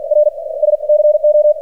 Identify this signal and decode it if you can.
Morse code, heard as a keyed tone, about 46 wpm, spaced as for 28 wpm, 600 Hz, with radio noise. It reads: RS CO
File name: archived / CRSnet